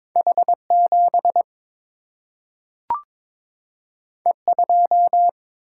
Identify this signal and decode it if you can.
Morse code, heard as a keyed tone, about 22 wpm, 700 Hz. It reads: H 7 E 2
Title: Single-Letter Single-Number 22wpm